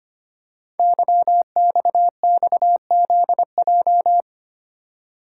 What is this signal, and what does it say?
Morse code, heard as a keyed tone, about 25 wpm, 700 Hz. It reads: YXXZJ
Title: Code Groups - 5 Letters - 25wpm